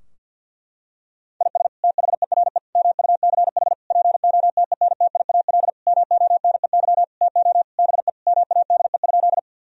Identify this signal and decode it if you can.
Morse code, heard as a keyed tone, about 50 wpm, 700 Hz. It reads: IS THERE MUCH POTENTIAL MONEY TO BE MADE?
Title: Sentences from Top 1000 Words - Review Part 4 - 50wpm